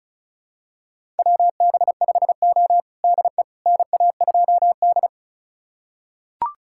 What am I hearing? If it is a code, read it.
Morse code, heard as a keyed tone, about 36 wpm, 700 Hz. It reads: WB5O DE NA2D
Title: USA Callsign DE Callsign - 4 Characters - 35wpm